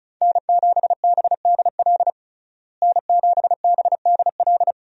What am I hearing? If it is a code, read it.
Morse code, heard as a keyed tone, about 36 wpm, 700 Hz. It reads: N7BDL N7BDL
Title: USA Call Signs - 5 Characters - Repeated 3x - 35wpm